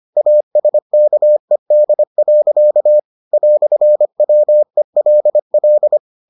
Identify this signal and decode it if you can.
Morse code, heard as a keyed tone, about 25 wpm, 600 Hz. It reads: ASKED. "WELL